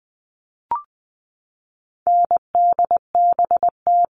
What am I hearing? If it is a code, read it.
Morse code, heard as a keyed tone, about 20 wpm, 700 Hz. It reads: NDBT
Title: All Cut Numbers - 4-Digit 20wpm